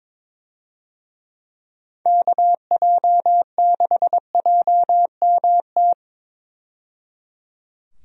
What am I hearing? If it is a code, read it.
Morse code, heard as a keyed tone, about 22 wpm, 700 Hz. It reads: KJ6JMT
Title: USA Call Signs - 6 Characters - 22wpm